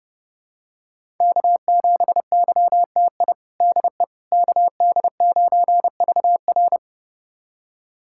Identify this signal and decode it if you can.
Morse code, heard as a keyed tone, about 30 wpm, 700 Hz. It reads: K7YTI DE KD9VR
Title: USA Callsign DE Callsign - 5 Characters - 30wpm